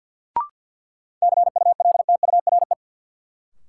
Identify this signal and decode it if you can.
Morse code, heard as a keyed tone, about 52 wpm, 700 Hz. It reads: CULTURE
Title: Sentences from Top 1000 Words - Review Part 6 - New Word Drill - 50wpm